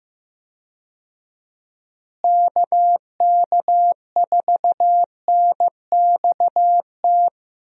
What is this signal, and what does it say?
Morse code, heard as a keyed tone, about 15 wpm, 700 Hz. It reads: KK4NXT
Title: USA Call Signs - 6 Characters - Repeated 2x - 15wpm